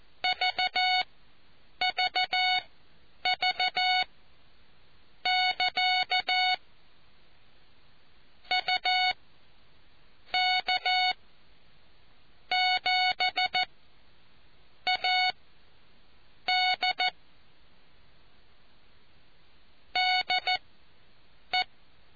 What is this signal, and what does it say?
Morse code, heard as a keyed tone, about 14 wpm, spaced as for 6 wpm, 740 Hz. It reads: VVV <KA> UK7AD DE